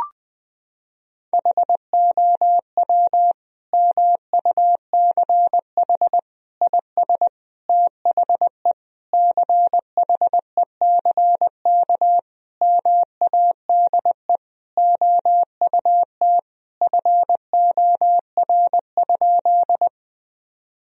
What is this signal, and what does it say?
Morse code, heard as a keyed tone, about 20 wpm, 700 Hz. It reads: HOW MUCH IS THE CHECK MADE OUT FOR?
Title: Sentences from Top 1000 Words - Review Part 3 - 20wpm